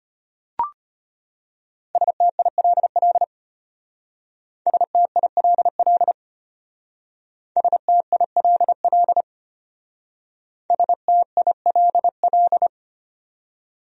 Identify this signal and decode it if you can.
Morse code, heard as a keyed tone, about 31 wpm, 700 Hz. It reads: STILL STILL STILL STILL
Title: Speed-Racing Top 500 Words 25wpm